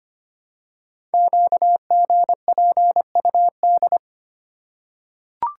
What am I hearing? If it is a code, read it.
Morse code, heard as a keyed tone, about 25 wpm, 700 Hz. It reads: QGPUD
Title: Code Groups - 5 Letters - 25wpm